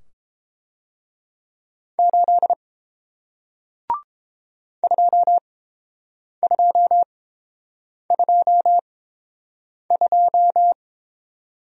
Morse code, heard as a keyed tone, about 29 wpm, spaced as for 7 wpm, 700 Hz. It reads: 8 2222